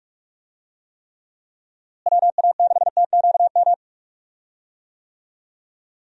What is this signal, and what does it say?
Morse code, heard as a keyed tone, about 45 wpm, 700 Hz. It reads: WA6TQM
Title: USA Call Signs - 6 Characters - Repeated 2x - 45wpm